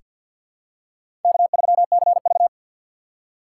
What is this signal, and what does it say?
Morse code, heard as a keyed tone, about 47 wpm, 700 Hz. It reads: K3CV
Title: USA Call Signs - 4 Characters - 6x Character Spacing - 50wpm